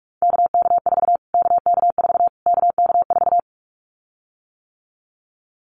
Morse code, heard as a keyed tone, about 44 wpm, 700 Hz. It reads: KK4 KK4 KK4
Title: Callsign Prefixes for Top 25 DXCC - Repeated 3x - 45wpm